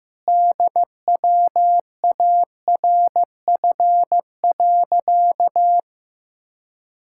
Morse code, heard as a keyed tone, about 15 wpm, 700 Hz. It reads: DWARF.
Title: Pangram Challenge v2 - 15wpm